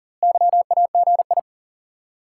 Morse code, heard as a keyed tone, about 40 wpm, 700 Hz. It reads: YAGI
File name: IQE Course 3 - Lesson 89 - YAGI - Two Elements - 40wpm